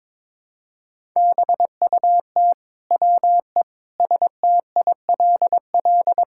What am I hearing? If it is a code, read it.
Morse code, heard as a keyed tone, about 22 wpm, 700 Hz. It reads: BUT WE STILL
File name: Sentences with 9-Word Phrases - 22wpm